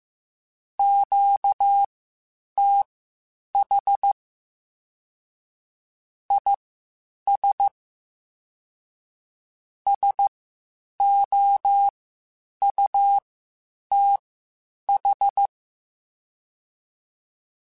Morse code, heard as a keyed tone, about 15 wpm, spaced as for 8 wpm, 800 Hz. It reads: QTH IS SOUTH